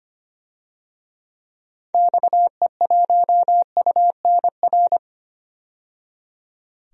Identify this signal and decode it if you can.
Morse code, heard as a keyed tone, about 25 wpm, 700 Hz. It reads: XE1UNR
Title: DXCC-100 Call Signs - 6 Characters - 25wpm